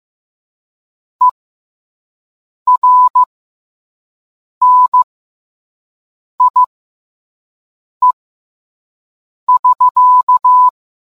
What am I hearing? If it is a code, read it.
Morse code, heard as a keyed tone, about 15 wpm, spaced as for 5 wpm, 1000 Hz. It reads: ERNIE<SK>